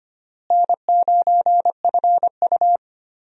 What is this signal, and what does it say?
Morse code, heard as a keyed tone, about 25 wpm, 700 Hz. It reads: N9FU
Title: USA Call Signs - 4 Characters - Repeated 3x - 25wpm